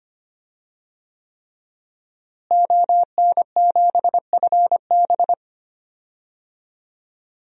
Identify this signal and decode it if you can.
Morse code, heard as a keyed tone, about 25 wpm, 700 Hz. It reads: ON7FB
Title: DXCC-100 Call Signs - 5 Characters - Repeated 3x - 25wpm